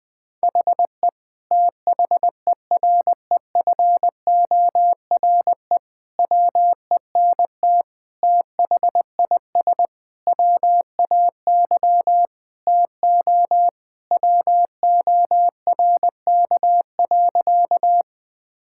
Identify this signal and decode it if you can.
Morse code, heard as a keyed tone, about 21 wpm, 700 Hz. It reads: HE THEREFORE WENT THIS WAY TO WORK.
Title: Sentences from Top 1000 Words - Review Part 5 - 20wpm